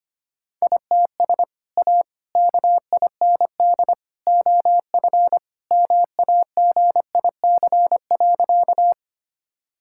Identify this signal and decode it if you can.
Morse code, heard as a keyed tone, about 25 wpm, 700 Hz. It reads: ITS A KIND OF MAGIC.
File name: practice / 25wpm / Sentences with 3-Word Phrases - 25wpm